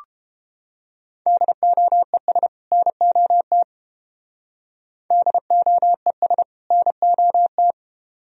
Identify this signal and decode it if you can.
Morse code, heard as a keyed tone, about 30 wpm, 700 Hz. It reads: DOES NOT DOES NOT